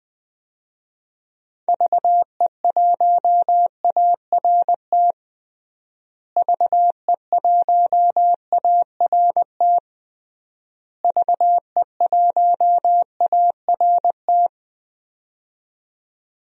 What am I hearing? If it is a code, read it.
Morse code, heard as a keyed tone, about 21 wpm, 700 Hz. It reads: VE1ART VE1ART VE1ART
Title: DXCC-100 Call Signs - 6 Characters - Repeated 3x - 20wpm